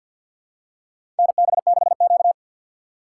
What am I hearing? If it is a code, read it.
Morse code, heard as a keyed tone, about 53 wpm, 700 Hz. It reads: NB6Q
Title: USA Call Signs - 4 Characters - 3x Character Spacing - 50wpm